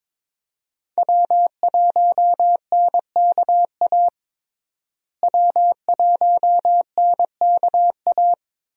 Morse code, heard as a keyed tone, about 22 wpm, 700 Hz. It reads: W1NKA W1NKA
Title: USA Call Signs - 5 Characters - Repeated 3x - 22wpm